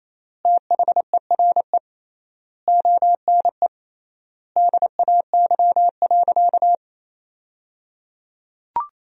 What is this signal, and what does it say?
Morse code, heard as a keyed tone, about 28 wpm, 700 Hz. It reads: THERE ONE DAY.
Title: 3x Word-Spacing - Sentences from Top 300 Words - 28wpm